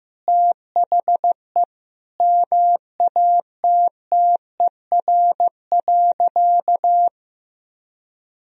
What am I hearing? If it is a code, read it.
Morse code, heard as a keyed tone, about 16 wpm, 700 Hz. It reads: THE MATTER.
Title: Sentences from Top 1000 Words - Review Part 3 - 15wpm